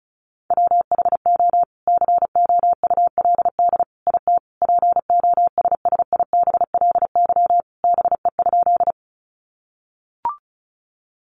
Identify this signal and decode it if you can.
Morse code, heard as a keyed tone, about 34 wpm, 700 Hz. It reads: WHO COULD IT POSSIBLY BE?
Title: Sentences from Top 1300 Words 35wpm